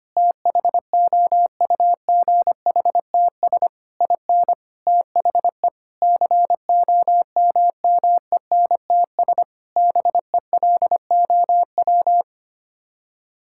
Morse code, heard as a keyed tone, about 25 wpm, 700 Hz. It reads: THOUGHTS IN THE COMMENTS BELOW
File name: Sets of 9 Words - 25wpm